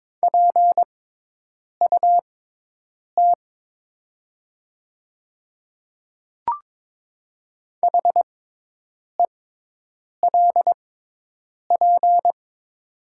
Morse code, heard as a keyed tone, about 22 wpm, spaced as for 8 wpm, 700 Hz. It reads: PUT HELP
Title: 6x Character Spacing - Top 300 Words - 22wpm